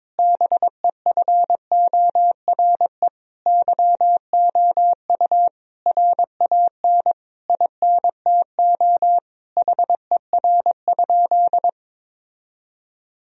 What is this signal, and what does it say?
Morse code, heard as a keyed tone, about 22 wpm, 700 Hz. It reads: BEFORE YOU RAN INTO HER?
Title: Sentences from Top 1000 Words - Review Part 3 - 22wpm